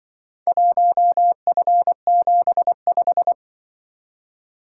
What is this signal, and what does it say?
Morse code, heard as a keyed tone, about 23 wpm, 700 Hz. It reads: 1F75
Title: Speed-Racing - Four Letter-Number Combinations - 20wpm